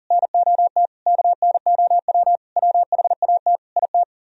Morse code, heard as a keyed tone, about 50 wpm, 700 Hz. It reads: NOT KNOW WHAT IT